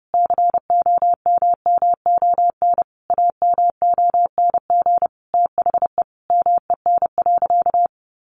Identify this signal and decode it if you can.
Morse code, heard as a keyed tone, about 30 wpm, 700 Hz. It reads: COMMON AMONG THE MEN.
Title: Sentences from Top 1400 Words 30wpm